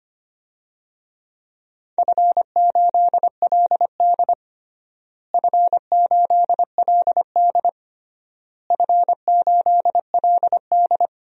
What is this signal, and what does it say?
Morse code, heard as a keyed tone, about 25 wpm, 700 Hz. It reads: F8LD F8LD F8LD
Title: DXCC-100 Call Signs - 4 Characters - Repeated 3x - 25wpm